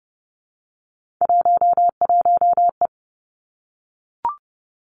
Morse code, heard as a keyed tone, about 30 wpm, 700 Hz. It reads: 11E
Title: Three Challenging Letter-Number Combinations 30wpm